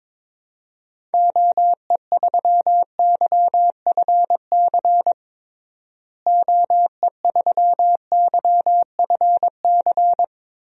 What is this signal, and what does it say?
Morse code, heard as a keyed tone, about 22 wpm, 700 Hz. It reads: OE3YFC OE3YFC
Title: DXCC-100 Call Signs - 6 Characters - Repeated 2x - 22wpm